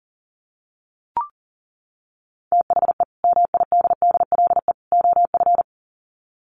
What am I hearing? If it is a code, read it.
Morse code, heard as a keyed tone, about 40 wpm, 700 Hz. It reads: THE MIDDLE OF
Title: Sets of 3 Words 40wpm